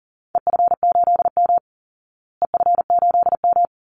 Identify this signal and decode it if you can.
Morse code, heard as a keyed tone, about 40 wpm, 700 Hz. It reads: EF8M EF8M
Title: Call Signs Repeated 3x - 40wpm